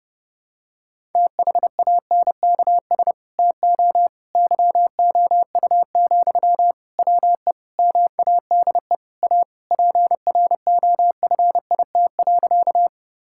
Morse code, heard as a keyed tone, about 30 wpm, 700 Hz. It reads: THANKS TO YOU, WE MADE A PROFIT.
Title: Sentences from Top 1000 Words - Review Part 5 - 30wpm